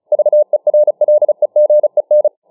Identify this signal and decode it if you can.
Morse code, heard as a keyed tone, about 35 wpm, 600 Hz. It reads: VERLEGEN